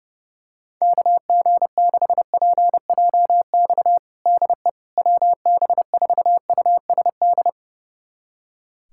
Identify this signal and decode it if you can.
Morse code, heard as a keyed tone, about 30 wpm, 700 Hz. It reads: KG6PJX DE WB4USD
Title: USA Callsign DE Callsign - 6 Characters - 30wpm